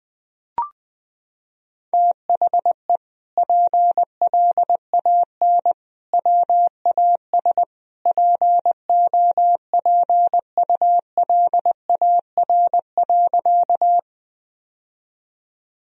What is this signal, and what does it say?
Morse code, heard as a keyed tone, about 20 wpm, 700 Hz. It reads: THE PLAN WAS POPULAR.
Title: Sentences from Top 1000 Words - Review Part 3 - 20wpm